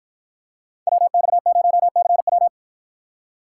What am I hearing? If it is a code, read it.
Morse code, heard as a keyed tone, about 53 wpm, 700 Hz. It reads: WX0CW